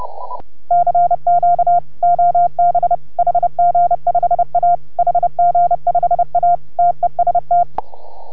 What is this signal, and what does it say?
Morse code, heard as a keyed tone, about 30 wpm, 690 Hz. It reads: CQ OB HG5A HG5A TEST